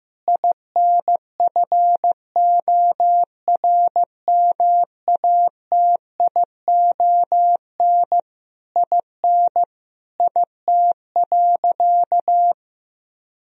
Morse code, heard as a keyed tone, about 16 wpm, 700 Hz. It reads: INFORMATION IN IT.
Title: Sentences from Top 1000 Words - Review Part 4 - 15wpm